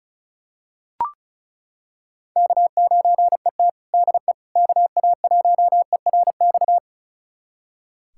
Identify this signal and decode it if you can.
Morse code, heard as a keyed tone, about 34 wpm, 700 Hz. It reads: K9ET DE KA1ERX
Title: USA Callsign DE Callsign 35wpm